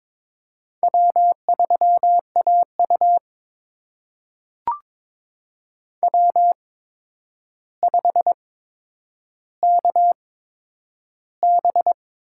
Morse code, heard as a keyed tone, about 22 wpm, spaced as for 9 wpm, 700 Hz. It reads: W3AU W5KB
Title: USA Call Signs - 4 Characters - 8x Character Spacing - 22wpm